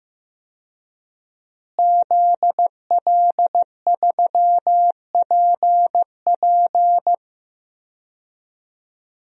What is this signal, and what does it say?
Morse code, heard as a keyed tone, about 15 wpm, 700 Hz. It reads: ZL3PP
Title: DXCC-100 Call Signs - 5 Characters - Repeated 2x - 15wpm